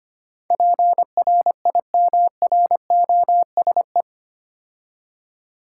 Morse code, heard as a keyed tone, about 25 wpm, 700 Hz. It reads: PRIMROSE
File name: practice / 25wpm / Difficult Words with Hints - 25wpm